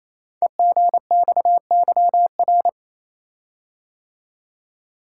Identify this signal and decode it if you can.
Morse code, heard as a keyed tone, about 27 wpm, 700 Hz. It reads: EGXYR
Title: Code Groups - 5 Letters - 28wpm